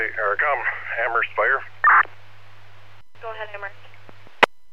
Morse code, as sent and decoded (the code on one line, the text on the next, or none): none